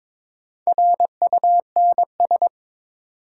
Morse code, heard as a keyed tone, about 22 wpm, 700 Hz. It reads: RUNS
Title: Sentences from Top 600 Words - New Word Drill - 22wpm